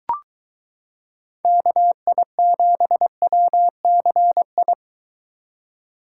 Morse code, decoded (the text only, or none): KI7WCI